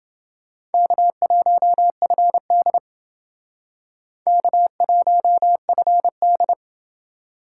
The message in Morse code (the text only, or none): K1FD K1FD